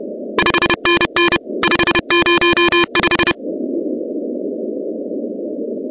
Morse code, heard as a keyed tone, about 30 wpm, 2400 Hz, with noise. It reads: 5NN 505
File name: contests / 2019-cw-wpx-kt5j-k5pi- -2330 0-17